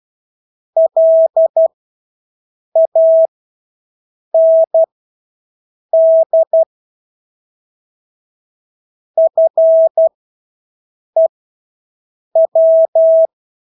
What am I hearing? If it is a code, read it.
Morse code, heard as a keyed tone, about 12 wpm, spaced as for 6 wpm, 650 Hz. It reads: LAND FEW
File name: Engelske ord 12-6 wpm